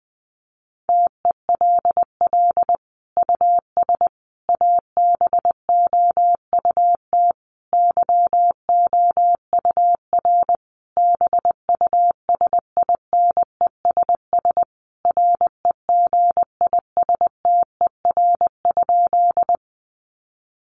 Morse code, heard as a keyed tone, about 20 wpm, 700 Hz. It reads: TELL US ABOUT YOUR BUSINESS REGISTER?
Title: Sentences from Top 1200 Words 20wpm